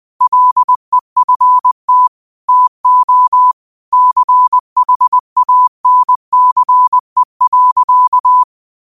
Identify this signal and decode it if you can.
Morse code, heard as a keyed tone, about 20 wpm, 1000 Hz. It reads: LEFT TO CHANCE.